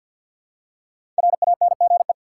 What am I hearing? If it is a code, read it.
Morse code, heard as a keyed tone, about 52 wpm, 700 Hz. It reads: RANGE
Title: Five Letter Words - 50wpm